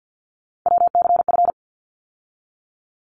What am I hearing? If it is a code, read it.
Morse code, heard as a keyed tone, about 50 wpm, 700 Hz. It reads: WCF
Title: ARRL Sections 50wpm